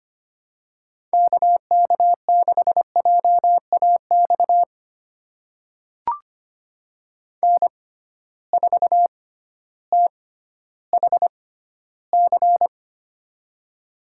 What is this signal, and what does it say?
Morse code, heard as a keyed tone, about 25 wpm, spaced as for 14 wpm, 700 Hz. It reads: KK6JAX N4THC
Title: USA Call Signs - 6x Character Spacing - 25wpm